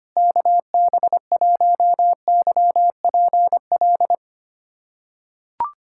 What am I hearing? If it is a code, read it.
Morse code, heard as a keyed tone, about 25 wpm, 700 Hz. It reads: KB1YPL
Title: USA Callsign DE Callsign - 6 Characters - 25wpm